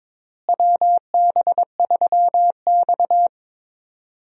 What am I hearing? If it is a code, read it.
Morse code, heard as a keyed tone, about 22 wpm, 700 Hz. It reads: WB3X
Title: USA Call Signs - 4 Characters - Repeated 3x - 22wpm